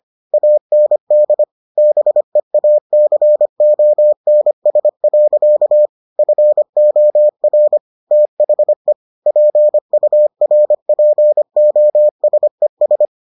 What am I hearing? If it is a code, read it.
Morse code, heard as a keyed tone, about 25 wpm, 600 Hz. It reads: AND BEACONS. FOR THE PURPOSES